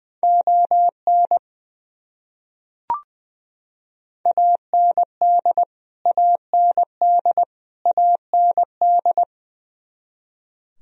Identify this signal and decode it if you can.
Morse code, heard as a keyed tone, about 20 wpm, 700 Hz. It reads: ON AND AND AND
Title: Top 100 Words Repeated 20wpm